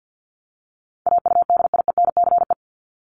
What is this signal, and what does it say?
Morse code, heard as a keyed tone, about 48 wpm, 700 Hz. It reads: AUDIENCE
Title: Top 751-1000 Words 50wpm